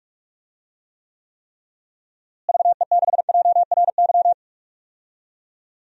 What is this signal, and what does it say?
Morse code, heard as a keyed tone, about 44 wpm, 700 Hz. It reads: VE6JRY
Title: DXCC-100 Call Signs - 6 Characters - Repeated 3x - 45wpm